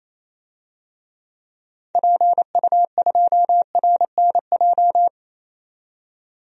Morse code, heard as a keyed tone, about 27 wpm, 700 Hz. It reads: PU2RNJ
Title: DXCC-100 Call Signs - 6 Characters - 28wpm